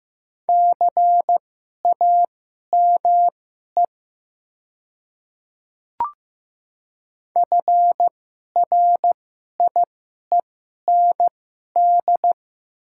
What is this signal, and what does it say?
Morse code, heard as a keyed tone, about 15 wpm, spaced as for 11 wpm, 700 Hz. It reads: CAME FRIEND
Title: 2x Character Spacing - Top 500 Words - 15wpm